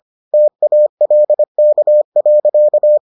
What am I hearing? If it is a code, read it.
Morse code, heard as a keyed tone, about 25 wpm, 600 Hz. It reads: TALK.